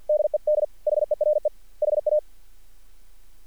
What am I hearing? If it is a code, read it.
Morse code, heard as a keyed tone, about 50 wpm, 600 Hz. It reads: DED HERE HA
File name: archived / CRSnet